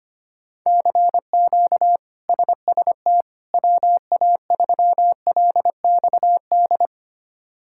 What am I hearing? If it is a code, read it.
Morse code, heard as a keyed tone, about 25 wpm, 700 Hz. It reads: CQ SST WA3LXD
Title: K1USN SST QSO Practice - 25wpm